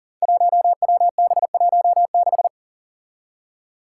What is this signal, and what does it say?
Morse code, heard as a keyed tone, about 40 wpm, 700 Hz. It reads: 1WB16
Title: Five Challenging Letter-Number Combinations 40wpm